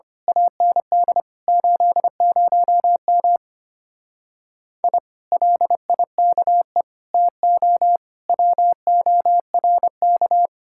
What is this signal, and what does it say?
Morse code, heard as a keyed tone, about 25 wpm, 700 Hz. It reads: AND 80M I LIKE TO WORK
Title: Speed-Racing Common QSO Phrases - 25wpm